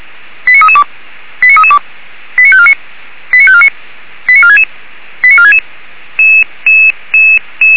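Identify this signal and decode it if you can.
Five-tone selective call (ZVEI), heard as quick runs of five tones, tones of about 70 ms. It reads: 89332 89332 89549 89549 89466 89466
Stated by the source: ZVEI five-tone call received by SDR, from the server of Feuerwehr Orlamünde (fire brigade).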